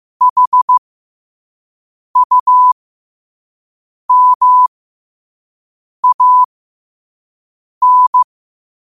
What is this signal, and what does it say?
Morse code, heard as a keyed tone, about 15 wpm, spaced as for 5 wpm, 1000 Hz. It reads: HUMAN
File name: qotdfiles / 5wpm